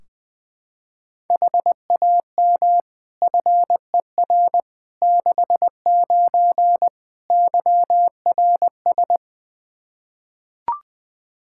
Morse code, heard as a keyed tone, about 21 wpm, 700 Hz. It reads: HAM FER 69 YRS
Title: K1USN SST QSO Practice 20wpm